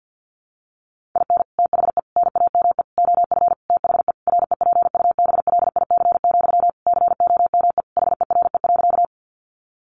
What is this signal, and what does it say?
Morse code, heard as a keyed tone, about 50 wpm, 700 Hz. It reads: IN THE NAME OF THE REPUBLIC, COME HERE.
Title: Sentences from Top 1500 Words 50wpm